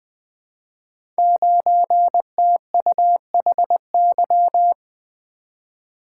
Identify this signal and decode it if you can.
Morse code, heard as a keyed tone, about 20 wpm, 700 Hz. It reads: 9TUHY
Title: Code Groups - 5 Letter:Numbers - 20wpm